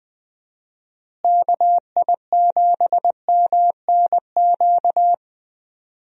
Morse code, heard as a keyed tone, about 20 wpm, 700 Hz. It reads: KI7MNQ
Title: USA Call Signs - 6 Characters - Repeated 2x - 20wpm